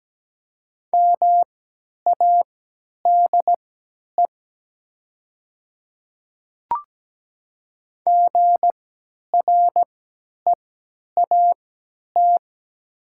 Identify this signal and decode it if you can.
Morse code, heard as a keyed tone, about 17 wpm, spaced as for 10 wpm, 700 Hz. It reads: MADE GREAT
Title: 3x Character Spacing - Top 200 Words - 17wpm